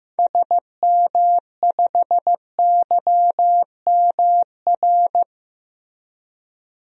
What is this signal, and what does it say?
Morse code, heard as a keyed tone, about 15 wpm, 700 Hz. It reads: SM5YMR
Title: DXCC-100 Call Signs - 6 Characters - Repeated 3x - 15wpm